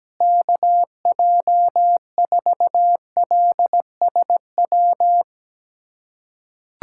Morse code, heard as a keyed tone, about 17 wpm, 700 Hz. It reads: KJ4LSW
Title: USA Callsign DE Callsign 17wpm